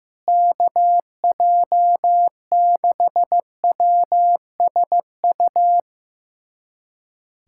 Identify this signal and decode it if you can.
Morse code, heard as a keyed tone, about 15 wpm, 700 Hz. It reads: KJ6WSU
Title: USA Callsign DE Callsign 15wpm